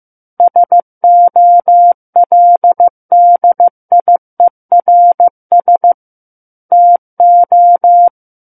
Morse code, heard as a keyed tone, about 15 wpm, 700 Hz. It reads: SOLDIERS TO
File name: Book One / 06-Chapter 6 - 15 WPM